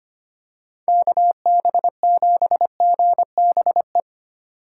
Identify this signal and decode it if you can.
Morse code, heard as a keyed tone, about 25 wpm, 700 Hz. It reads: KB7GBE
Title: USA Call Signs - 6 Characters - 2x Character Spacing - 25wpm